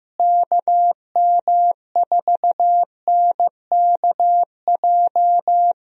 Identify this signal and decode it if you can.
Morse code, heard as a keyed tone, about 15 wpm, 700 Hz. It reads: KM4NKJ
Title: USA Call Signs - 6 Characters - Repeated 2x - 15wpm